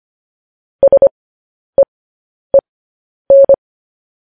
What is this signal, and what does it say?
Morse code, heard as a keyed tone, about 25 wpm, spaced as for 10 wpm, 550 Hz. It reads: SEEN
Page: News Headlines in Morse Code at 10 WPM